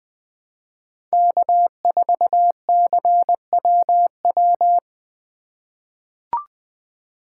Morse code, heard as a keyed tone, about 20 wpm, 700 Hz. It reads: K4CWW
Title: USA Call Signs - 5 Characters - 3x Character Spacing - 20wpm